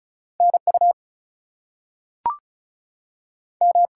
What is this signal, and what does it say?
Morse code, heard as a keyed tone, about 36 wpm, 700 Hz. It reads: NU M
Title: US States and Canadian Provinces 35wpm